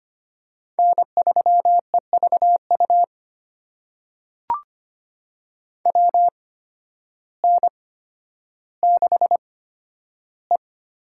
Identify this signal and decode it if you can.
Morse code, heard as a keyed tone, about 25 wpm, spaced as for 12 wpm, 700 Hz. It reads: N3EVU WN6E
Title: USA Call Signs - 8x Character Spacing - 25wpm